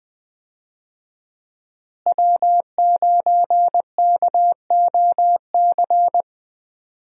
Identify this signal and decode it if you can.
Morse code, heard as a keyed tone, about 20 wpm, 700 Hz. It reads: W9KOC
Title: USA Call Signs - 5 Characters - Repeated 3x - 20wpm